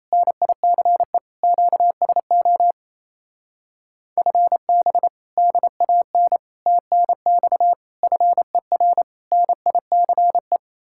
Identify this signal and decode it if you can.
Morse code, heard as a keyed tone, about 32 wpm, 700 Hz. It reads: NICE QSO FB DAN TNX FER NICE
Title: Speed-Racing Common QSO Phrases Encore - 28wpm